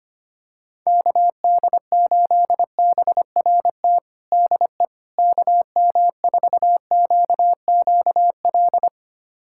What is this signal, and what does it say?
Morse code, heard as a keyed tone, about 25 wpm, 700 Hz. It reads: KD8BRT DE KM4QQL
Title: USA Callsign DE Callsign 25wpm